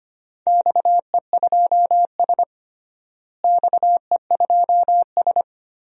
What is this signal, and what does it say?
Morse code, heard as a keyed tone, about 25 wpm, 700 Hz. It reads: XE2S XE2S
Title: DXCC-100 Call Signs - 4 Characters - Repeated 3x - 25wpm